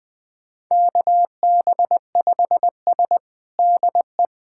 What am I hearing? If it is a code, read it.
Morse code, heard as a keyed tone, about 20 wpm, 700 Hz. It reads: KB5S DE
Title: USA Callsign DE Callsign - 4 Characters - 20wpm